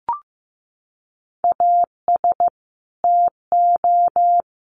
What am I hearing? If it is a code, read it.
Morse code, heard as a keyed tone, about 15 wpm, 700 Hz. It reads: AS TO
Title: Sets of 5 Words Encore 15wpm